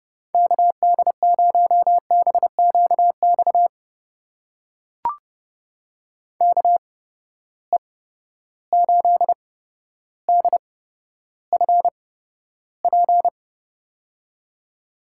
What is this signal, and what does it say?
Morse code, heard as a keyed tone, about 32 wpm, spaced as for 13 wpm, 700 Hz. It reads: KD0BQX KE8DFP
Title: USA Call Signs - 6 Characters - 8x Character Spacing - 30wpm